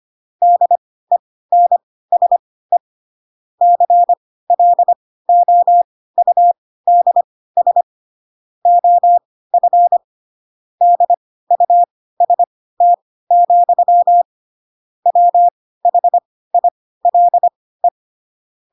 Morse code, heard as a keyed tone, about 25 wpm, spaced as for 16 wpm, 700 Hz. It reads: DENSE CLOUDS OF DUST, WHILE
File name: Pinocchio / Pinocchio ch6 10 25 ik1hge